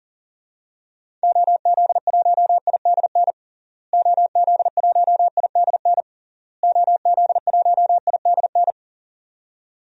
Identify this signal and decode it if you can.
Morse code, heard as a keyed tone, about 40 wpm, 700 Hz. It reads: OZ1IDN OZ1IDN OZ1IDN
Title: DXCC-100 Call Signs - 6 Characters - Repeated 3x - 40wpm